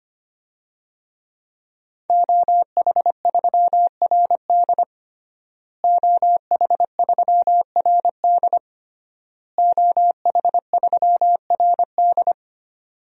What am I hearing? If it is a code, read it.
Morse code, heard as a keyed tone, about 25 wpm, 700 Hz. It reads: OH3RD OH3RD OH3RD
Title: DXCC-100 Call Signs - 5 Characters - Repeated 3x - 25wpm